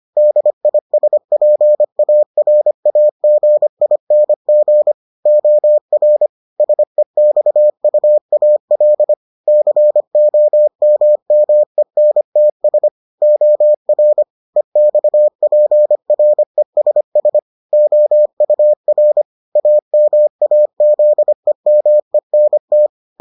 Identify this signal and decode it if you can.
Morse code, heard as a keyed tone, about 25 wpm, 600 Hz. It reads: DISPARAGING OR SEXUAL COMMENTS OR EXPRESS OUR AMAZEMENT